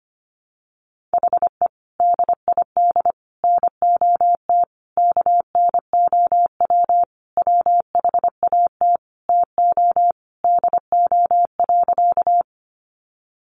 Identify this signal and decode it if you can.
Morse code, heard as a keyed tone, about 25 wpm, 700 Hz. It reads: HE DID NOT KNOW WHAT TO DO.
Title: Sentences from Top 600 Words 25wpm